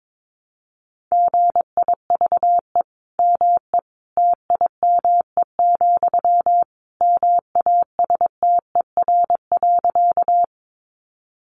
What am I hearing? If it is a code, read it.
Morse code, heard as a keyed tone, about 22 wpm, 700 Hz. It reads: GIVE ME TIME, MASTER.
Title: Sentences from Top 1300 Words 22wpm